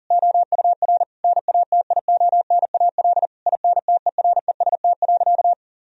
Morse code, heard as a keyed tone, about 37 wpm, 700 Hz. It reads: OUR NATIONAL INTEREST.
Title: Sentences from Top 1000 Words - Review Part 4 - 40wpm